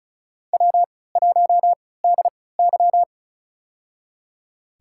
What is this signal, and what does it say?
Morse code, heard as a keyed tone, about 34 wpm, spaced as for 20 wpm, 700 Hz. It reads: W1DY
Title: USA Call Signs - 4 Characters - 3x Character Spacing - 35wpm